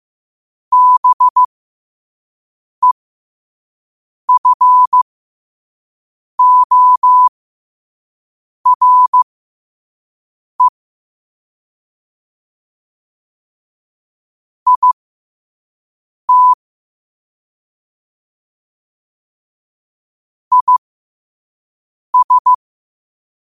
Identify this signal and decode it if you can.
Morse code, heard as a keyed tone, about 15 wpm, spaced as for 5 wpm, 1000 Hz. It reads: BEFORE IT IS
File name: qotdfiles / 5wpm